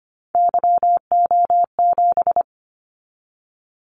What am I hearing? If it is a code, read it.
Morse code, heard as a keyed tone, about 25 wpm, 700 Hz. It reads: YO7
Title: Callsign Prefixes for Top 25 DXCC - 25wpm